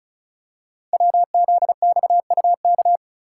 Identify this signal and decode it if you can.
Morse code, heard as a keyed tone, about 36 wpm, 700 Hz. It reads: WZXUK
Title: Code Groups - 5 Letter:Numbers - 35wpm